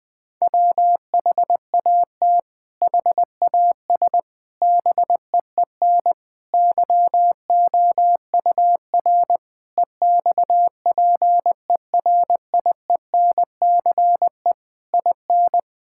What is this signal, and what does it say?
Morse code, heard as a keyed tone, about 21 wpm, 700 Hz. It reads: WHAT HAS BEEN YOUR EXPERIENCE IN
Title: Sentences from Top 1000 Words - Review Part 6 - 20wpm